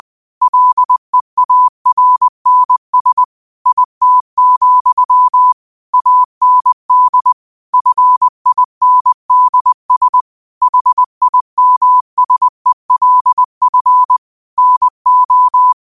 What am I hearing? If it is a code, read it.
Morse code, heard as a keyed tone, about 20 wpm, 1000 Hz. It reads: LEARNS IT, AND FINDS HIMSELF NO
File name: qotdfiles / 20wpm